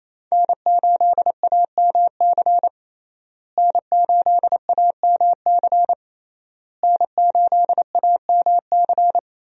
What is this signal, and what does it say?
Morse code, heard as a keyed tone, about 28 wpm, 700 Hz. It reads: N8AMC N8AMC N8AMC
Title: USA Call Signs - 5 Characters - Repeated 3x - 28wpm